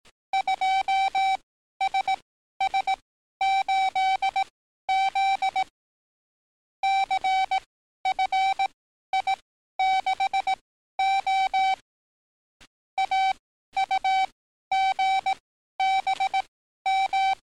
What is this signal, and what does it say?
Morse code, heard as a keyed tone, about 18 wpm, spaced as for 12 wpm, 770 Hz. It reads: 2SS8Z CFI6O AUGBM
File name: training / NAVPERS 11049to11060 / NAVPERS 11052 RA RB